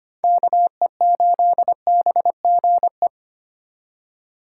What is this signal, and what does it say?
Morse code, heard as a keyed tone, about 25 wpm, 700 Hz. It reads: KE8BGE
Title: USA Call Signs - 6 Characters - 4x Character Spacing - 25wpm